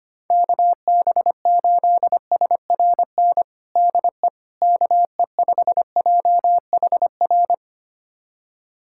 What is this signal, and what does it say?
Morse code, heard as a keyed tone, about 25 wpm, 700 Hz. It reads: KB8SRN DE KE5JHR
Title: USA Callsign DE Callsign - 6 Characters - 25wpm